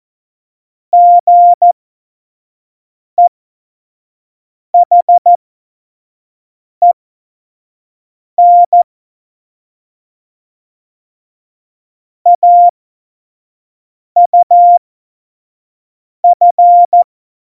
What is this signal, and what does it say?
Morse code, heard as a keyed tone, about 14 wpm, spaced as for 5 wpm, 700 Hz. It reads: GEHEN AUF
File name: Morsekurs / Witze / Witz 13. November 2018, 25 F70